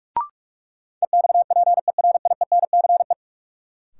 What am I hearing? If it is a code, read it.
Morse code, heard as a keyed tone, about 46 wpm, 700 Hz. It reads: EXPERIENCE
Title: Sentences from Top 600 Words - New Word Drill - 45wpm